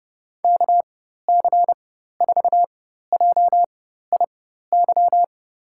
Morse code, heard as a keyed tone, about 30 wpm, spaced as for 14 wpm, 700 Hz. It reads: KC4JIY
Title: USA Call Signs - 6 Characters - 4x Character Spacing - 30wpm